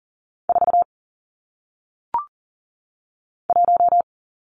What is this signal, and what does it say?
Morse code, heard as a keyed tone, about 39 wpm, 700 Hz. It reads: 4 1
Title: Numbers - 1-Digit 40wpm